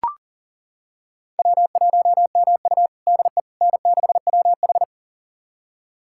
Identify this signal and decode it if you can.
Morse code, heard as a keyed tone, about 38 wpm, 700 Hz. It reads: W1MU DE N6WH
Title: USA Callsign DE Callsign - 4 Characters - 40wpm